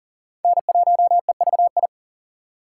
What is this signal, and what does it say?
Morse code, heard as a keyed tone, about 40 wpm, 700 Hz. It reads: N1EVI
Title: USA Call Signs - 5 Characters - 6x Character Spacing - 40wpm